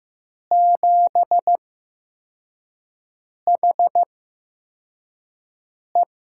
Morse code, heard as a keyed tone, about 15 wpm, spaced as for 4 wpm, 700 Hz. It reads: 7HE
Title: USA Call Signs - 4 Characters - 8x Character Spacing - 15wpm